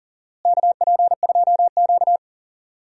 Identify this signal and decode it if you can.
Morse code, heard as a keyed tone, about 37 wpm, 700 Hz. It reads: KP2Q
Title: DXCC-100 Call Signs - 4 Characters - Repeated 2x - 40wpm